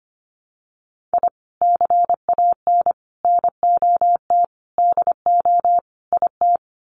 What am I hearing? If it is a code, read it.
Morse code, heard as a keyed tone, about 25 wpm, 700 Hz. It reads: I CAN NOT DO IT
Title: Sets of 5 Words Encore 25wpm